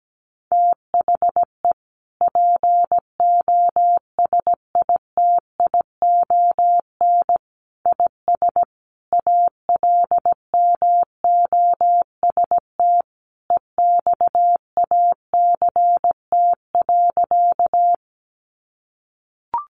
Morse code, heard as a keyed tone, about 17 wpm, 700 Hz. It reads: THE POSITION IS ALMOST EXACT.
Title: Sentences from Top 1300 Words 17wpm